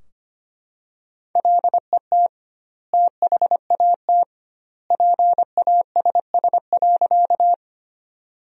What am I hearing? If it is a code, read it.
Morse code, heard as a keyed tone, about 25 wpm, 700 Hz. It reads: LET THAT PASS.
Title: 2x Word-Spacing - Sentences from Top 400 Words - 25wpm